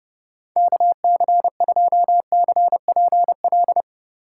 Morse code, heard as a keyed tone, about 30 wpm, 700 Hz. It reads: KC2CPL
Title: USA Call Signs - 4x Character Spacing - 30wpm